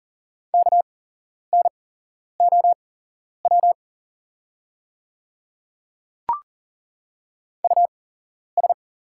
Morse code, heard as a keyed tone, about 40 wpm, spaced as for 11 wpm, 700 Hz. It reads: KNOW US
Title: 8x Character Spacing - Top 200 Words - 40wpm